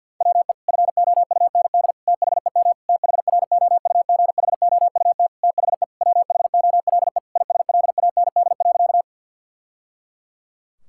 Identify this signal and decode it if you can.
Morse code, heard as a keyed tone, about 51 wpm, 700 Hz. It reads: WE FOUND THEM THROUGHOUT THE WHOLE ISLAND.
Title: Sentences from Top 1000 Words - Review Part 5 - 50wpm